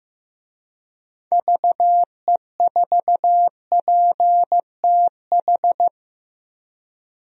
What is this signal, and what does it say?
Morse code, heard as a keyed tone, about 15 wpm, 700 Hz. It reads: VE4PTH
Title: DXCC-100 Call Signs - 6 Characters - Repeated 2x - 15wpm